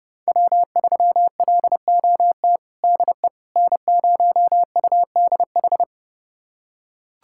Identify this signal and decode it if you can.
Morse code, heard as a keyed tone, about 32 wpm, 700 Hz. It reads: W3LOT DE N0UDH
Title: USA Callsign DE Callsign - 5 Characters - 30wpm